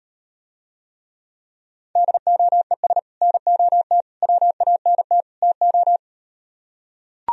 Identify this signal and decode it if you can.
Morse code, heard as a keyed tone, about 38 wpm, 700 Hz. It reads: DOES NOT WANT TO